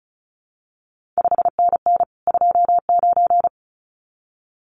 Morse code, heard as a keyed tone, about 36 wpm, 700 Hz. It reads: 5NN 29
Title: CQ WW DX Contest Exchange Encore - 35wpm